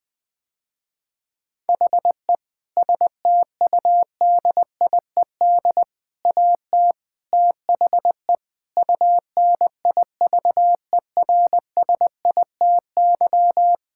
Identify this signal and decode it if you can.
Morse code, heard as a keyed tone, about 20 wpm, 700 Hz. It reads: HE STUDIED AT THE UNIVERSITY
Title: Sentences with 4-Word Phrases - Spoken Beginning - 20wpm